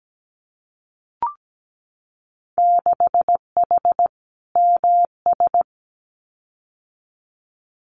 Morse code, heard as a keyed tone, about 17 wpm, 700 Hz. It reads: 6H MS
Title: Winter Field Day Exchanges Encore 17wpm